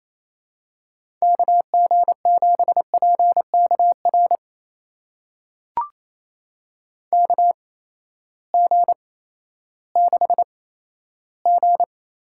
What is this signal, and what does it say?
Morse code, heard as a keyed tone, about 28 wpm, spaced as for 14 wpm, 700 Hz. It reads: KG7PKR KG6G
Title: USA Call Signs - 6 Characters - 8x Character Spacing - 28wpm